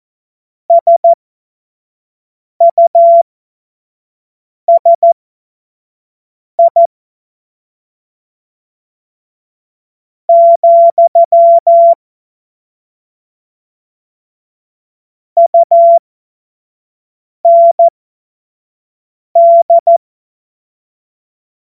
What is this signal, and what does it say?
Morse code, heard as a keyed tone, about 14 wpm, spaced as for 4 wpm, 680 Hz. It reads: SUSI, UND
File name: Morsekurs / Witze / Witz 21. Februar 2023 25 F70